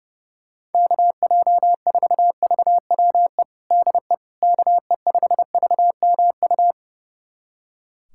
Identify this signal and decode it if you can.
Morse code, heard as a keyed tone, about 30 wpm, 700 Hz. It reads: KJ4VWE DE KE5VMU
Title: USA Callsign DE Callsign - 6 Characters - 30wpm